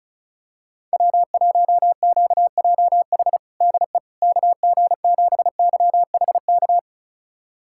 Morse code, heard as a keyed tone, about 36 wpm, 700 Hz. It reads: W1QJH DE KG7YHK
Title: USA Callsign DE Callsign 35wpm